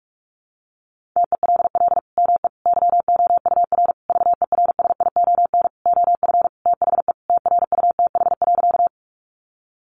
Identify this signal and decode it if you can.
Morse code, heard as a keyed tone, about 44 wpm, 700 Hz. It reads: TELL ME YOUR VERSION OF THE TRUTH.
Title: Sentences from Top 1400 Words 45wpm